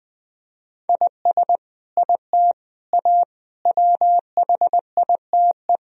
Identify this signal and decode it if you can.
Morse code, heard as a keyed tone, about 20 wpm, 700 Hz. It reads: IS IT A WHITE
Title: Sentences from Top 1000 Words - Review Part 1 - 20wpm